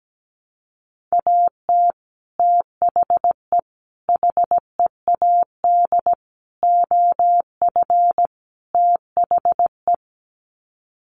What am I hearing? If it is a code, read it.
Morse code, heard as a keyed tone, about 17 wpm, 700 Hz. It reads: AT THE HEAD OF THE
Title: Sets of 5 Words 17wpm